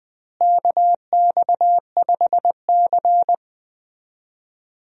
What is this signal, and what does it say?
Morse code, heard as a keyed tone, about 20 wpm, 700 Hz. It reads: KX5C
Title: USA Call Signs - 4 Characters - Repeated 2x - 20wpm